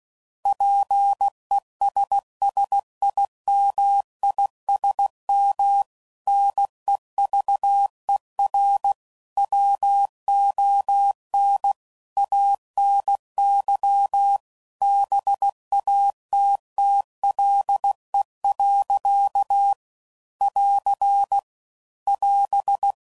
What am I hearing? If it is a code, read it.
Morse code, heard as a keyed tone, about 16 wpm, 800 Hz. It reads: PESSIMISM NEVER WON ANY BATTLE. <AR> <AS>